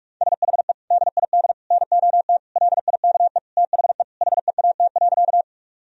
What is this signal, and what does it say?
Morse code, heard as a keyed tone, about 46 wpm, 700 Hz. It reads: SHE DID NOT LIKE THE HEAT.
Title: Sentences from Top 1000 Words - Review Part 3 - 45wpm